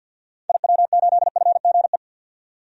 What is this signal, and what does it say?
Morse code, heard as a keyed tone, about 50 wpm, 700 Hz. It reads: IW8FGE
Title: DXCC-100 Call Signs - 6 Characters - 50wpm